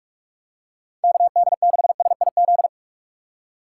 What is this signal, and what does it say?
Morse code, heard as a keyed tone, about 46 wpm, 700 Hz. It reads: KD6SIZ
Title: USA Call Signs - 6 Characters - 3x Character Spacing - 45wpm